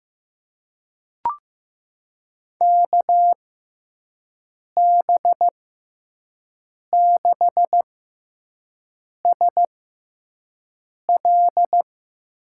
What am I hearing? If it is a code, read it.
Morse code, heard as a keyed tone, about 15 wpm, spaced as for 5 wpm, 700 Hz. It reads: KB6SL
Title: USA Call Signs - 6x Character Spacing - 15wpm